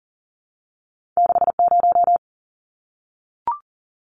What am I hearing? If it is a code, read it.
Morse code, heard as a keyed tone, about 40 wpm, 700 Hz. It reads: 60
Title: Numbers - 2-Digit 40wpm